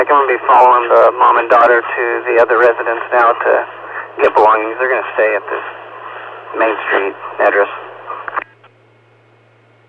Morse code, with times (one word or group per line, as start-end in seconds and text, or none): none